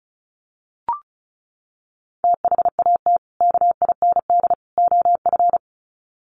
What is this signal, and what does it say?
Morse code, heard as a keyed tone, about 34 wpm, 700 Hz. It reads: THAT KIND OF
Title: Sets of 3 Words 35wpm